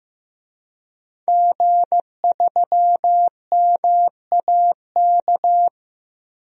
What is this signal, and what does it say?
Morse code, heard as a keyed tone, about 15 wpm, 700 Hz. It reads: G3MAK
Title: DXCC-100 Call Signs - 5 Characters - Repeated 2x - 15wpm